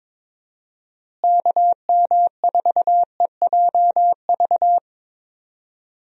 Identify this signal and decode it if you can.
Morse code, heard as a keyed tone, about 22 wpm, 700 Hz. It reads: KM4EJV
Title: USA Call Signs - 6 Characters - Repeated 2x - 22wpm